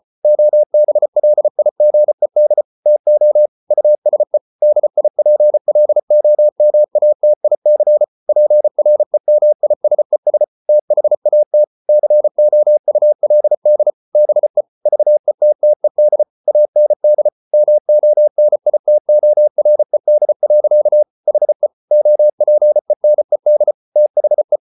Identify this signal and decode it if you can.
Morse code, heard as a keyed tone, about 34 wpm, 600 Hz. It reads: OBLIGED TO USE DIPLOMATIC PREMISES THAT COULD BE VETTED AND MONITORED. HE OPENED THE